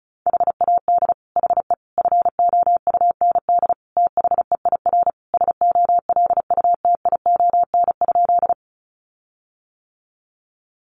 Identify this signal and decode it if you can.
Morse code, heard as a keyed tone, about 34 wpm, 700 Hz. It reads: HAD HE FOUND THEIR SOLUTION?
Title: Sentences from Top 1300 Words 35wpm